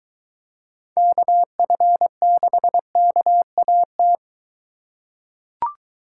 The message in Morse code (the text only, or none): KF6KAT